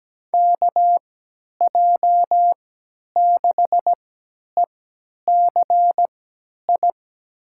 Morse code, heard as a keyed tone, about 17 wpm, spaced as for 10 wpm, 700 Hz. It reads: KJ6ECI
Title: USA Call Signs - 6 Characters - 3x Character Spacing - 17wpm